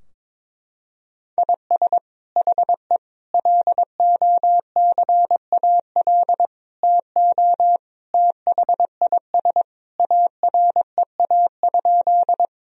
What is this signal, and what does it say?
Morse code, heard as a keyed tone, about 21 wpm, 700 Hz. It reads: IS HE LOCAL TO THIS AREA?
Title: Sentences from Top 1000 Words - Review Part 3 - 22wpm